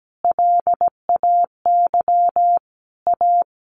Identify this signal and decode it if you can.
Morse code, heard as a keyed tone, about 17 wpm, 700 Hz. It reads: LAY A
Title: Sentences from Top 1200 Words 17wpm